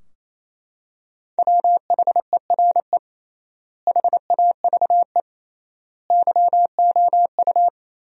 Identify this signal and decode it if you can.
Morse code, heard as a keyed tone, about 28 wpm, 700 Hz. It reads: WHERE HAVE YOU
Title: 3x Word-Spacing - Sentences from Top 300 Words - 28wpm